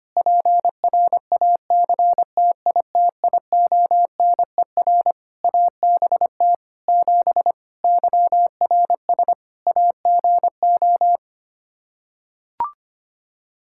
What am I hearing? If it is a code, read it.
Morse code, heard as a keyed tone, about 25 wpm, 700 Hz. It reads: PRACTITIONER ABT 7 YRS AGO
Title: Practice QSO - Exchange 3 - More Info - 25wpm